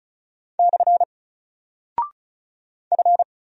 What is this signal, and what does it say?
Morse code, heard as a keyed tone, about 36 wpm, 700 Hz. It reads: / F
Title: Single Character (Letter:Number:Punctuation:Prosign) Alt 35wpm